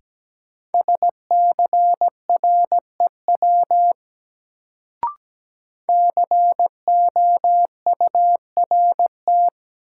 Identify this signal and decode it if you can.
Morse code, heard as a keyed tone, about 17 wpm, 700 Hz. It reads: SCREW COURT
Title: Sentences from Top 1600 Words - New Word Drill Alt - 17wpm